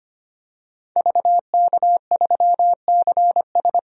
Morse code, heard as a keyed tone, about 25 wpm, 700 Hz. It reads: VK3CS
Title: DXCC-100 Call Signs - 5 Characters - Repeated 2x - 25wpm